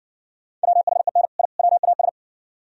Morse code, heard as a keyed tone, about 60 wpm, 700 Hz. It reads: WHEN I WAS